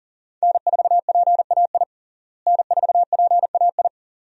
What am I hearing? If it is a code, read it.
Morse code, heard as a keyed tone, about 40 wpm, 700 Hz. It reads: N4PAI N4PAI
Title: USA Call Signs - 5 Characters - Repeated 2x - 40wpm